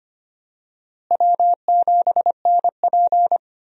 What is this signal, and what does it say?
Morse code, heard as a keyed tone, about 25 wpm, 700 Hz. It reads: W7NP
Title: USA Call Signs - 4 Characters - Repeated 3x - 25wpm